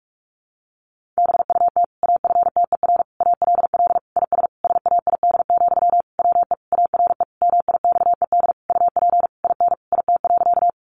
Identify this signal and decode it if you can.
Morse code, heard as a keyed tone, about 44 wpm, 700 Hz. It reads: BUT AFTER ALL IS SAID, WE ARE MIXED UP IN IT.
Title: Sentences from Top 1500 Words 45wpm